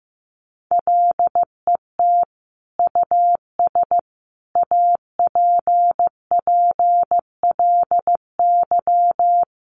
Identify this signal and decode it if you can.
Morse code, heard as a keyed tone, about 15 wpm, 700 Hz. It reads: LET US APPLY
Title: Sentences from Top 1200 Words 15wpm